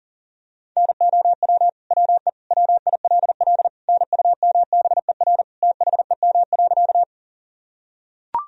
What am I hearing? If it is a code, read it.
Morse code, heard as a keyed tone, about 40 wpm, 700 Hz. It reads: NOW WE WILL NUMBER THEM.
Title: Sentences from Top 1000 Words - Review Part 1 - 40wpm